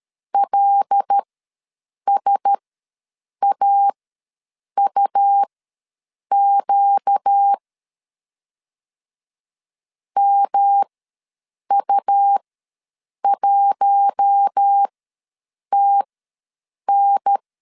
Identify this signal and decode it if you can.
Morse code, heard as a keyed tone, about 13 wpm, spaced as for 7 wpm, 800 Hz. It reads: LSAUQ MU1TN
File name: 5 LB ws 12wpm